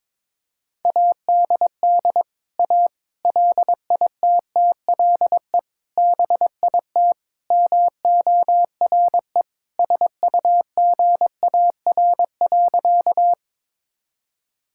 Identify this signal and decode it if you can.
Morse code, heard as a keyed tone, about 22 wpm, 700 Hz. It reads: ADD A LITTLE BIT MORE SUGAR.
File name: practice / 22wpm / Sentences with 4-Word Phrases - 22wpm